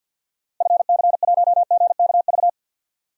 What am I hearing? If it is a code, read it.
Morse code, heard as a keyed tone, about 47 wpm, 700 Hz. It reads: FC1GKV
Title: DXCC-100 Call Signs - 6 Characters - Repeated 2x - 50wpm